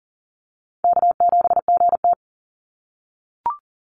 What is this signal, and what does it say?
Morse code, heard as a keyed tone, about 40 wpm, 700 Hz. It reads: K7GT
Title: Call Signs - 40wpm